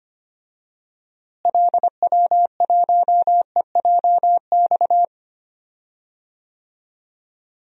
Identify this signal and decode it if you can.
Morse code, heard as a keyed tone, about 25 wpm, 700 Hz. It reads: LW1EJX
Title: DXCC-100 Call Signs - 6 Characters - Repeated 2x - 25wpm